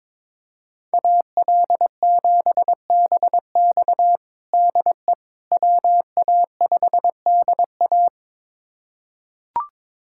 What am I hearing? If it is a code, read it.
Morse code, heard as a keyed tone, about 22 wpm, 700 Hz. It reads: AL7BX DE WA5DA
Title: USA Callsign DE Callsign - 5 Characters - 22wpm